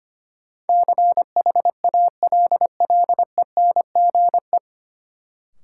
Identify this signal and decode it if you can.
Morse code, heard as a keyed tone, about 25 wpm, 700 Hz. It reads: CHALLENGE
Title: Sentences from Top 1000 Words - Review Part 7 - New Word Drill - 25wpm